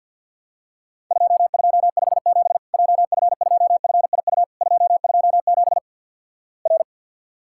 Morse code, heard as a keyed tone, about 50 wpm, 700 Hz. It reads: 2257 JF2FIU 227 R
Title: Worked All Europe DX Contest - QTC Exchange with Repeats - 50wpm